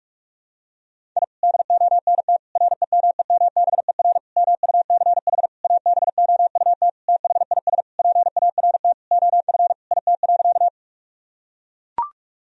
Morse code, heard as a keyed tone, about 42 wpm, 700 Hz. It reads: I DONT REMEMBER MUCH ABOUT THIS PART OF IT.
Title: Sentences from Top 1000 Words - Review Part 1 - 45wpm